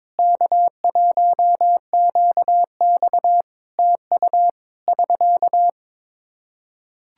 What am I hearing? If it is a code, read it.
Morse code, heard as a keyed tone, about 22 wpm, 700 Hz. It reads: K1QX TU <SK>
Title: Practice QSO - Exchange 4 - Ending - 22wpm